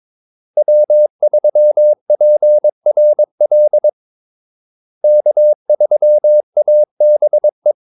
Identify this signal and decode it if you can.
Morse code, heard as a keyed tone, about 22 wpm, 600 Hz. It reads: W3PRL K3ABE
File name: Callsigns-22wpm